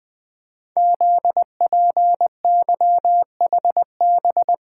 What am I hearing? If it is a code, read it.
Morse code, heard as a keyed tone, about 20 wpm, 700 Hz. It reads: ZPYHB
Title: Code Groups - 5 Letter:Numbers - 20wpm